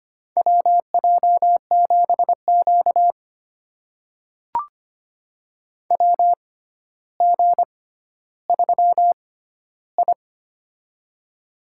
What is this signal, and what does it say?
Morse code, heard as a keyed tone, about 25 wpm, spaced as for 13 wpm, 700 Hz. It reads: WJ7Q WG3I
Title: USA Call Signs - 4 Characters - 6x Character Spacing - 25wpm